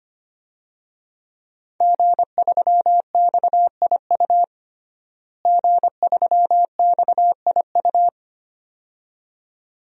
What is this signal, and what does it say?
Morse code, heard as a keyed tone, about 25 wpm, 700 Hz. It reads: G3XIU G3XIU
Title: DXCC-100 Call Signs - 5 Characters - Repeated 2x - 25wpm